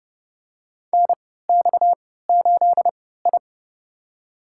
Morse code, heard as a keyed tone, about 30 wpm, spaced as for 17 wpm, 700 Hz. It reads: NX8I
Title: USA Call Signs - 4 Characters - 3x Character Spacing - 30wpm